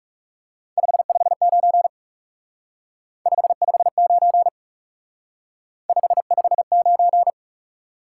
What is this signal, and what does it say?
Morse code, heard as a keyed tone, about 38 wpm, 700 Hz. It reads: 559 559 559
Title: Speed-Racing Top 180 QSO Elements 30wpm